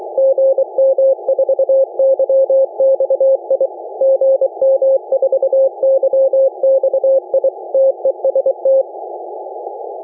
Morse code, heard as a keyed tone, about 23 wpm, 540 Hz. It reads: GM4YXI GM4YXI TEST